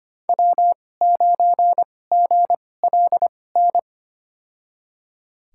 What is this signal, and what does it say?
Morse code, heard as a keyed tone, about 25 wpm, spaced as for 18 wpm, 700 Hz. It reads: W9GLN
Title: USA Call Signs - 5 Characters - 2x Character Spacing - 25wpm